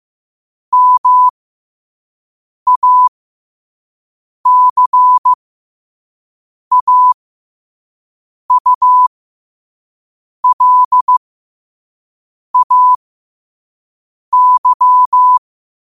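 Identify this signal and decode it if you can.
Morse code, heard as a keyed tone, about 15 wpm, spaced as for 5 wpm, 1000 Hz. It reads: MACAULAY
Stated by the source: Morse quote of the day at 5 wpm.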